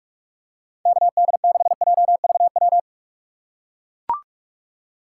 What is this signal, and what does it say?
Morse code, heard as a keyed tone, about 44 wpm, 700 Hz. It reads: KD6JVW
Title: USA Call Signs - 4x Character Spacing - 45wpm